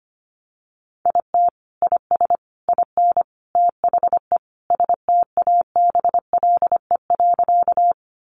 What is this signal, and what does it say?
Morse code, heard as a keyed tone, about 25 wpm, 700 Hz. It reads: IT IS IN THE STABLE.
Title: Sentences from Top 1500 Words 25wpm